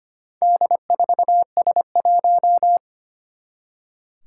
Morse code, heard as a keyed tone, about 25 wpm, 700 Hz. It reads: D4S1
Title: Four Challenging Letter-Number Combinations 25wpm